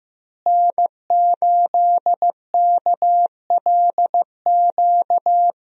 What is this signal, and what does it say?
Morse code, heard as a keyed tone, about 15 wpm, 700 Hz. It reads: N8KLQ
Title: USA Call Signs - 5 Characters - Repeated 3x - 15wpm